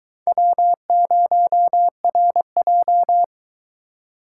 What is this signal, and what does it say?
Morse code, heard as a keyed tone, about 22 wpm, 700 Hz. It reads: W0RJ